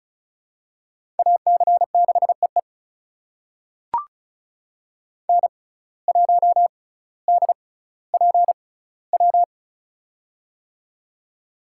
Morse code, heard as a keyed tone, about 36 wpm, spaced as for 18 wpm, 700 Hz. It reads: AC6EE N1DPW
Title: USA Call Signs - 5 Characters - 6x Character Spacing - 35wpm